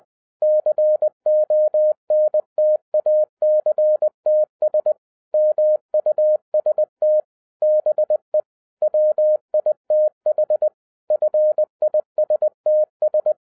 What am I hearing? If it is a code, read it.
Morse code, heard as a keyed tone, about 20 wpm, 610 Hz. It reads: CONTACTS MUST BE WITH FISTS
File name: 2012-01-mp3 / n15-INVESTIGATION INTO THE POSSIBILITY OF WINNING PREFIX VERSION 3 AWARD-20wpm